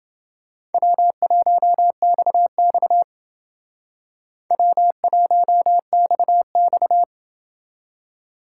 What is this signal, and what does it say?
Morse code, heard as a keyed tone, about 29 wpm, 700 Hz. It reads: W1XX W1XX